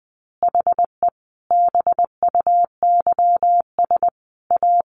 Morse code, heard as a keyed tone, about 20 wpm, 700 Hz. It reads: HE BUYS A
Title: Easy Phrases 20wpm